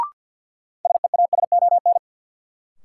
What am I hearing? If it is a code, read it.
Morse code, heard as a keyed tone, about 52 wpm, 700 Hz. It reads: SEASON
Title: Sentences from Top 1400 Words - New Word Drill - 50wpm